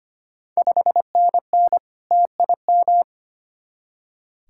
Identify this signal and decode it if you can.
Morse code, heard as a keyed tone, about 25 wpm, 700 Hz. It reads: 5NN TIM
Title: Ontario QSO Party Exchange 25wpm